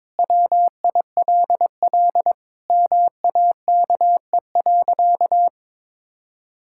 Morse code, heard as a keyed tone, about 22 wpm, 700 Hz. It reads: WILL MAKE.
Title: Sentences from Top 1000 Words - Review Part 5 - 22wpm